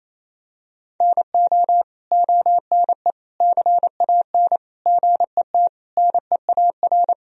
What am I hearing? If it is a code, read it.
Morse code, heard as a keyed tone, about 28 wpm, 700 Hz. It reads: NO ONE CAN GET NEAR
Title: Sentences from Top 1000 Words - Review Part 1 - 28wpm